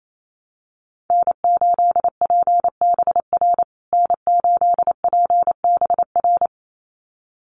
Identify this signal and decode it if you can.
Morse code, heard as a keyed tone, about 28 wpm, 700 Hz. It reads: N8PBR N8PBR
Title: USA Call Signs Repeated 2x 28wpm